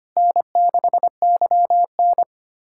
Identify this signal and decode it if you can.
Morse code, heard as a keyed tone, about 25 wpm, 700 Hz. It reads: N6YN
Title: USA Callsign DE Callsign - 4 Characters - 25wpm